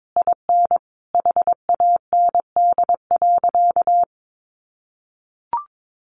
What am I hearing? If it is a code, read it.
Morse code, heard as a keyed tone, about 22 wpm, 700 Hz. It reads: IN HAND.
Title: Sentences from Top 1200 Words 22wpm